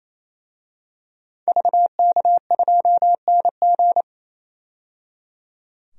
Morse code, heard as a keyed tone, about 28 wpm, 700 Hz. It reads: VK2NG
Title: DXCC-100 Call Signs - 5 Characters - 28wpm